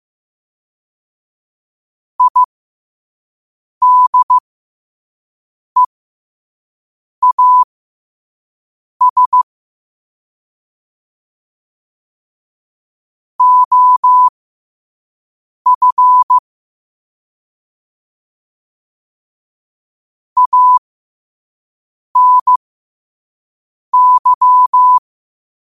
Morse code, heard as a keyed tone, about 15 wpm, spaced as for 5 wpm, 1000 Hz. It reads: IDEAS OF ANY
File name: 5wpm